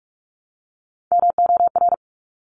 Morse code, heard as a keyed tone, about 46 wpm, 700 Hz. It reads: MOR
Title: Top Three Letter Combinations in Words 45wpm